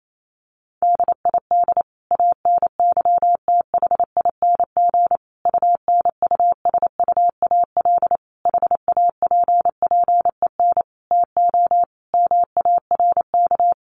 Morse code, heard as a keyed tone, about 28 wpm, 700 Hz. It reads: DID ANYTHING UNUSUAL HAPPEN TO MARK
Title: Sentences from Top 1400 Words 28wpm